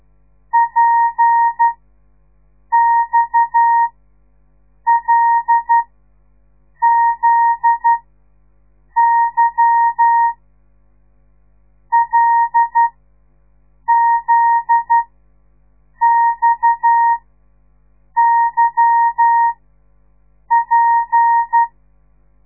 Morse code, heard as a keyed tone, about 12 wpm, spaced as for 6 wpm, 930 Hz. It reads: PXLZY LZXYP
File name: radiomorningstar / morse / les 8